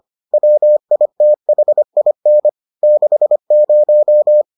Morse code, heard as a keyed tone, about 25 wpm, 600 Hz. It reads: WITHIN 60